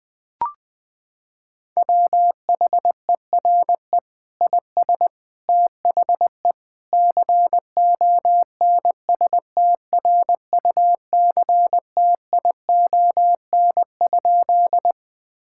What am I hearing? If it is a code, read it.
Morse code, heard as a keyed tone, about 21 wpm, 700 Hz. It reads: WHERE IS THE CONSTRUCTION?
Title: Sentences from Top 1600 Words 20wpm